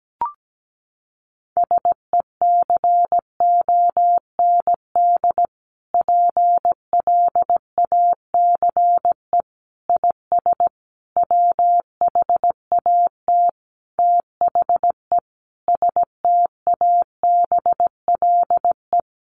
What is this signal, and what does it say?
Morse code, heard as a keyed tone, about 17 wpm, 700 Hz. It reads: SECOND PLACE IS WHAT THE STABLE
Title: Sentences from Top 1500 Words 17wpm